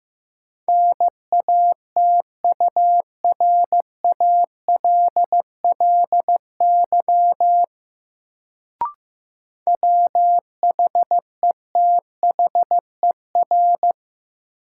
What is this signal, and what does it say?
Morse code, heard as a keyed tone, about 15 wpm, 700 Hz. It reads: NATURALLY WHETHER
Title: Sentences from Top 1300 Words - New Word Drill Alt - 15wpm